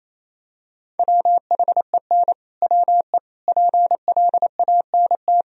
Morse code, heard as a keyed tone, about 28 wpm, 700 Hz. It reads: WHEN WE PLANT
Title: Sentences from Top 1000 Words - Review Part 1 - 28wpm